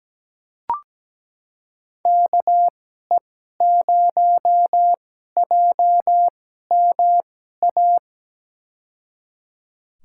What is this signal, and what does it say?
Morse code, heard as a keyed tone, about 17 wpm, spaced as for 12 wpm, 700 Hz. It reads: KE0JMA
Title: USA Call Signs - 6 Characters - 2x Character Spacing - 17wpm